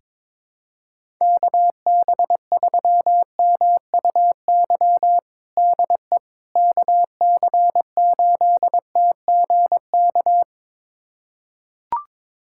Morse code, heard as a keyed tone, about 22 wpm, 700 Hz. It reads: KB3MUY DE KC8TGK
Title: USA Callsign DE Callsign - 6 Characters - 22wpm